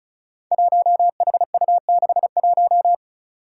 Morse code, heard as a keyed tone, about 36 wpm, 700 Hz. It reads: 1HU61
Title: Five Challenging Letter-Number Combinations 35wpm